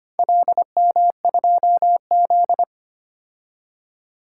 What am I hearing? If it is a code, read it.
Morse code, heard as a keyed tone, about 25 wpm, 700 Hz. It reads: LM2Z
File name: practice / 25wpm / Four Letter-Number Combinations - 25wpm